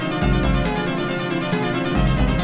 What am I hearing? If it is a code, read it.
There is no code.